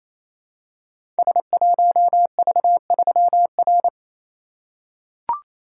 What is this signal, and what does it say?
Morse code, heard as a keyed tone, about 29 wpm, 700 Hz. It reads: S1V3R
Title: Code Groups - 5 Letter:Numbers - 28wpm